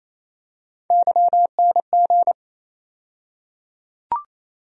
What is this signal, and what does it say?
Morse code, heard as a keyed tone, about 28 wpm, 700 Hz. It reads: YNG
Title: USA Call Sign Postfixes - 3 Letters 28wpm